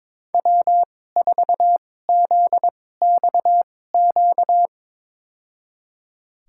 Morse code, heard as a keyed tone, about 21 wpm, spaced as for 16 wpm, 700 Hz. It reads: W4ZXQ
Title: USA Call Signs - 5 Characters - 2x Character Spacing - 22wpm